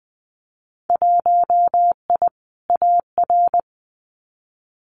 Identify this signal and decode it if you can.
Morse code, heard as a keyed tone, about 20 wpm, 700 Hz. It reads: 1I AR
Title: Winter Field Day Exchanges Encore 20wpm